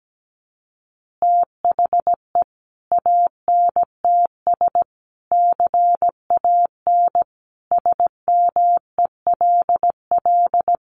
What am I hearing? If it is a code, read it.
Morse code, heard as a keyed tone, about 17 wpm, 700 Hz. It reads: THE ANTS CAN SMELL
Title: Additional Easy Phrases 17wpm